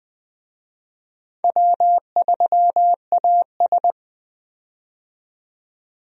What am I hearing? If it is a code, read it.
Morse code, heard as a keyed tone, about 20 wpm, 700 Hz. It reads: W3AS
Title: USA Call Signs - 4 Characters - 20wpm